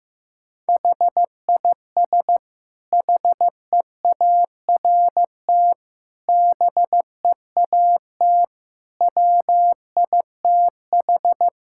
Morse code, heard as a keyed tone, about 15 wpm, 700 Hz. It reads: HIS HEART BEAT WITH
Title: Sentences from Top 1600 Words 15wpm